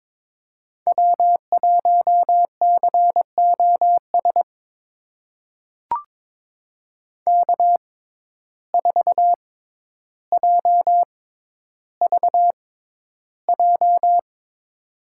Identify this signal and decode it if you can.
Morse code, heard as a keyed tone, about 23 wpm, spaced as for 11 wpm, 700 Hz. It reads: W1COS K4JVJ
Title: USA Call Signs - 5 Characters - 6x Character Spacing - 22wpm